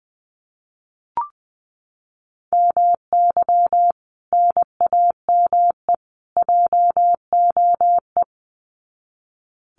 Morse code, heard as a keyed tone, about 20 wpm, 700 Hz. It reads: MY NAME JOE
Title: Common QSO Phrases 20wpm